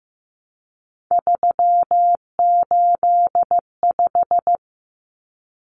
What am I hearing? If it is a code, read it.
Morse code, heard as a keyed tone, about 15 wpm, 700 Hz. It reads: 385
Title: Common Cut Numbers - 3-Digit 15wpm